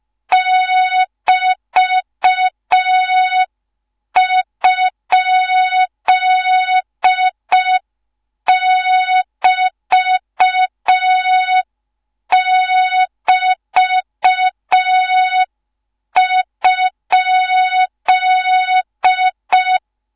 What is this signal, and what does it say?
Morse code, heard as a keyed tone, about 5 wpm, 3700 Hz. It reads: <BT>?<BT><BT>?